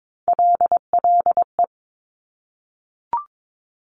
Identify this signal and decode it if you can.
Morse code, heard as a keyed tone, about 22 wpm, 700 Hz. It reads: LLE
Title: Top Three Letter Combinations in Words 22wpm